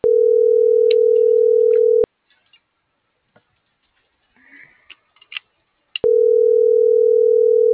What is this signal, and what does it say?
Ringback tone (North America), 440 + 480 Hz.